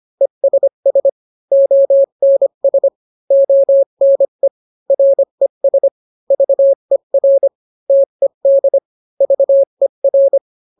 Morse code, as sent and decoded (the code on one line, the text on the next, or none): . ... ... / --- -. ... / --- -. / .-. ... / ...- . .-. / - . -.. / ...- . .-.
ESS ONS ONE RES VER TED VER